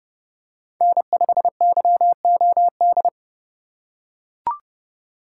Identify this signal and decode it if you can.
Morse code, heard as a keyed tone, about 30 wpm, 700 Hz. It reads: N5YOD